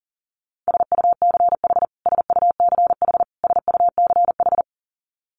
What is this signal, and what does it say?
Morse code, heard as a keyed tone, about 40 wpm, 700 Hz. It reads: SUCH SUCH SUCH
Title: Top 200 Words Repeated 40wpm